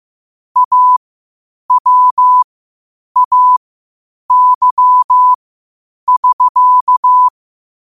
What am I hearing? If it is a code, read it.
Morse code, heard as a keyed tone, about 15 wpm, spaced as for 8 wpm, 1000 Hz. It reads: AWAY<SK>
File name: qotdfiles / 8wpm